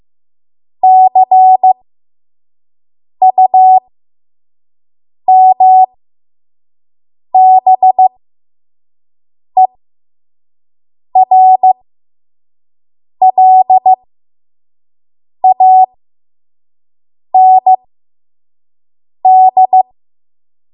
Morse code, heard as a keyed tone, about 15 wpm, spaced as for 5 wpm, 750 Hz. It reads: CUMBERLAND
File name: Morse / 05WPM